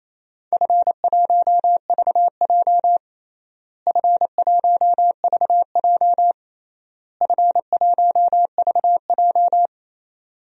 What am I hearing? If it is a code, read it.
Morse code, heard as a keyed tone, about 28 wpm, 700 Hz. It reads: F1VJ F1VJ F1VJ
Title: DXCC-100 Call Signs - 4 Characters - Repeated 3x - 28wpm